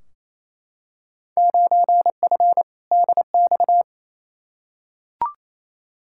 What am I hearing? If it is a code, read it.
Morse code, heard as a keyed tone, about 28 wpm, 700 Hz. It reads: 9F DX
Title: Field Day Exchanges Encore - 28wpm